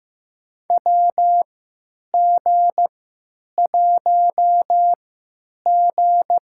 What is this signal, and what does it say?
Morse code, heard as a keyed tone, about 15 wpm, spaced as for 9 wpm, 700 Hz. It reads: WG1G
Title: USA Call Signs - 3x Character Spacing - 15wpm